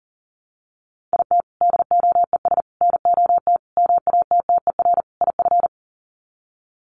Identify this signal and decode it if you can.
Morse code, heard as a keyed tone, about 40 wpm, 700 Hz. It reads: IT DOES NOT MATTER IF
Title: Sets of 5 Words Encore 40wpm